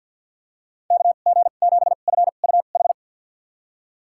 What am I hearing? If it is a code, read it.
Morse code, heard as a keyed tone, about 50 wpm, spaced as for 36 wpm, 700 Hz. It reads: KG7FUH
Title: USA Call Signs - 2x Character Spacing - 50wpm